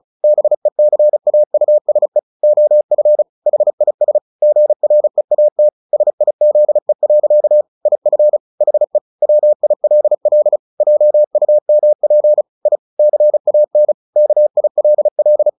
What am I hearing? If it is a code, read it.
Morse code, heard as a keyed tone, about 34 wpm, 600 Hz. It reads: BECAUSE OF HIS GREAT SIZE. IF HE WILL JUMP I CAN KILL
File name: the old man and the sea / the old man and the sea 1 35wpm